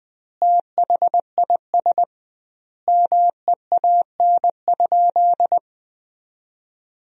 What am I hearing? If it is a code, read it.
Morse code, heard as a keyed tone, about 20 wpm, 700 Hz. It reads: THIS MEAN?
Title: 2x Word-Spacing - Sentences from Top 200 Words - 20wpm